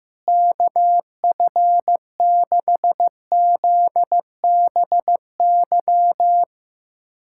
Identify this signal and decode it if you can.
Morse code, heard as a keyed tone, about 15 wpm, 700 Hz. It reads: KF6ZBY
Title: USA Callsign DE Callsign - 6 Characters - 15wpm